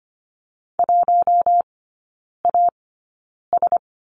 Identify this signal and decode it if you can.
Morse code, heard as a keyed tone, about 25 wpm, spaced as for 9 wpm, 700 Hz. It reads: 1AS
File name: Lesson 17e - Letter u - Three Characters 25wpm